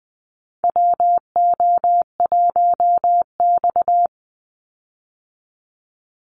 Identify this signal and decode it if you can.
Morse code, heard as a keyed tone, about 20 wpm, 700 Hz. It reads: WO1X
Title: Call Signs Repeated 2x - 20wpm